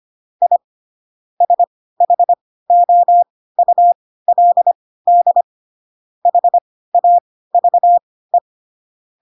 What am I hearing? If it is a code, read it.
Morse code, heard as a keyed tone, about 25 wpm, spaced as for 16 wpm, 700 Hz. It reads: I SHOULD HAVE
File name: Pinocchio / Pinocchio ch3 10 25 ik1hge